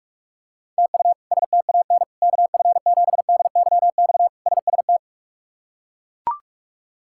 Morse code, heard as a keyed tone, about 41 wpm, 700 Hz. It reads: TU STAN KF7DYX SST
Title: K1USN SST QSO Practice - 45wpm